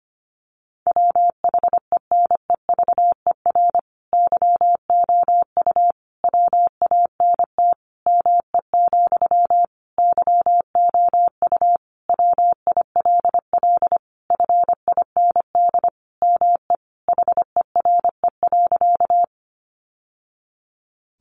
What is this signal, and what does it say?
Morse code, heard as a keyed tone, about 25 wpm, 700 Hz. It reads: WHENEVER YOU WANT ME, YOU WILL FIND ME HERE.
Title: Sentences from Top 1200 Words 25wpm